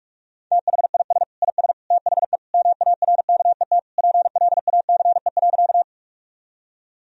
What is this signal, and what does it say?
Morse code, heard as a keyed tone, about 41 wpm, 700 Hz. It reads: THIS IS THE MARKET PLACE.
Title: Sentences from Top 1000 Words - Review Part 3 - 45wpm